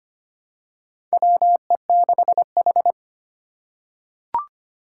Morse code, heard as a keyed tone, about 25 wpm, 700 Hz. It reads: WE6H
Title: USA Call Signs - 4 Characters - 2x Character Spacing - 25wpm